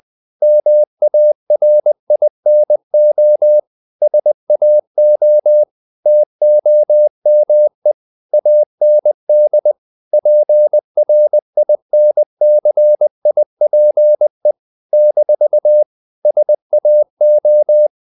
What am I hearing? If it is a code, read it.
Morse code, heard as a keyed tone, about 20 wpm, 600 Hz. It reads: MARINO SAO TOME AND PRINCIPE - SAO
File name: CW / Practice / NationalCapitals20